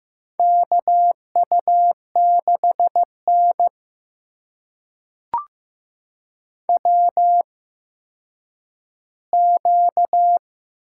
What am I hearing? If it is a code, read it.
Morse code, heard as a keyed tone, about 15 wpm, spaced as for 9 wpm, 700 Hz. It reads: KU6N WQ
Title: USA Call Signs - 4 Characters - 8x Character Spacing - 15wpm